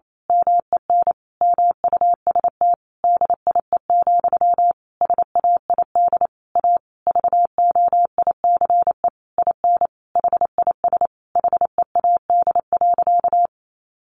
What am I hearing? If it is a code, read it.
Morse code, heard as a keyed tone, about 28 wpm, 700 Hz. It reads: MEN MUST DIE, SAID A VOICE IN HIS HEAD.
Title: Sentences from Top 1300 Words 28wpm